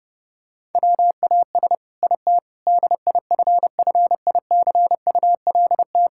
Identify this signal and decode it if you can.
Morse code, heard as a keyed tone, about 32 wpm, 700 Hz. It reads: WAS IT DIFFICULT
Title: Sentences from Top 1000 Words - Review Part 6 - 30wpm